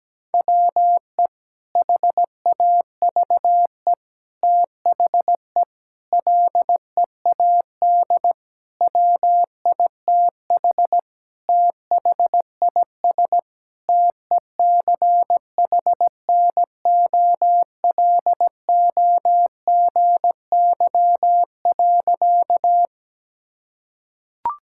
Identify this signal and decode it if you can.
Morse code, heard as a keyed tone, about 17 wpm, 700 Hz. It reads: WE HAVE THE LEAD WITH THIS TECHNOLOGY.
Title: Sentences from Top 1000 Words - Review Part 5 - 17wpm